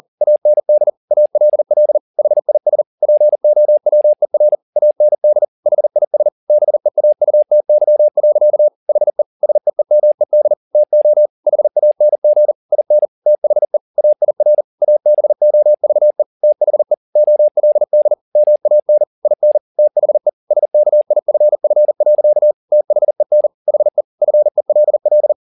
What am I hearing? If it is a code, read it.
Morse code, heard as a keyed tone, about 40 wpm, 600 Hz. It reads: AND ALL HIS POWER AND HIS BEAUTY. HE SEEMED TO HANG IN THE AIR ABOVE THE OLD MAN IN THE SKIFF. THEN HE FELL
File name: the old man and the sea 2 40wpm